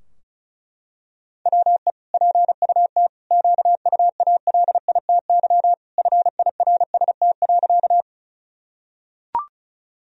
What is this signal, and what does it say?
Morse code, heard as a keyed tone, about 36 wpm, 700 Hz. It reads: WE PUT QUALITY FIRST.
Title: Sentences from Top 1000 Words - Review Part 4 - 35wpm